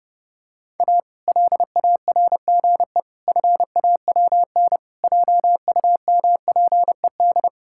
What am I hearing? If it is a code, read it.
Morse code, heard as a keyed tone, about 30 wpm, 700 Hz. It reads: A LARGE FAWN JUMPED
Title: Pangram Challenge v2 - 30wpm